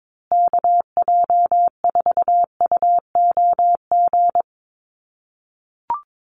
Morse code, heard as a keyed tone, about 22 wpm, 700 Hz. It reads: KJ4UOG
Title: USA Call Signs Repeated 3x 22wpm